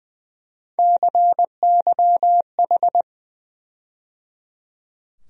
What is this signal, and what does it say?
Morse code, heard as a keyed tone, about 20 wpm, 700 Hz. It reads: CYH
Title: USA Call Sign Postfixes - 3 Letters 20wpm